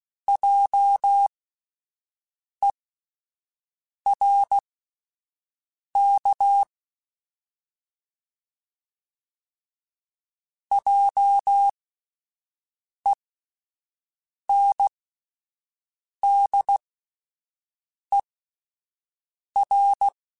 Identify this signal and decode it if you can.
Morse code, heard as a keyed tone, about 16 wpm, spaced as for 5 wpm, 800 Hz. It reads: JERK JENDER